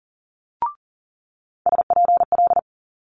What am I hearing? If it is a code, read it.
Morse code, heard as a keyed tone, about 39 wpm, 700 Hz. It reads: SPL
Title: Multi-letter Phonemes Quick Fire 40wpm